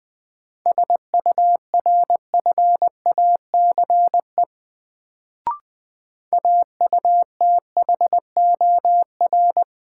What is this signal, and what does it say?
Morse code, heard as a keyed tone, about 20 wpm, 700 Hz. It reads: SURFACE AUTHOR
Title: Sets of 4 Words - New Word Drill - 20wpm